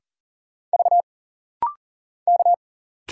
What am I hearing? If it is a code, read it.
Morse code, heard as a keyed tone, about 40 wpm, 700 Hz. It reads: V K
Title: Single Letter-Number - Rapid-Fire 40wpm